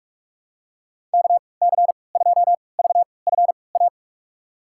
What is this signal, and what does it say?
Morse code, heard as a keyed tone, about 41 wpm, spaced as for 25 wpm, 700 Hz. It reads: KC2VFA
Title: USA Call Signs - 6 Characters - 3x Character Spacing - 45wpm